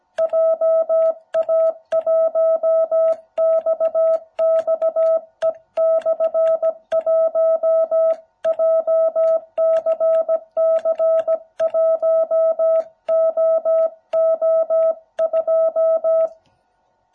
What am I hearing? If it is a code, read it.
Morse code, heard as a keyed tone, about 17 wpm, 660 Hz. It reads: JA1XXE/1 JCC1OO2